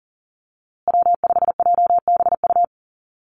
Morse code, heard as a keyed tone, about 37 wpm, 700 Hz. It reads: W5JBU
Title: USA Call Signs Repeated 3x 40wpm